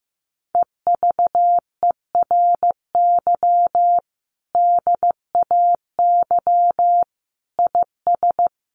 Morse code, heard as a keyed tone, about 14 wpm, 700 Hz. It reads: EVERY DAY IS
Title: Sentences from Top 1400 Words 15wpm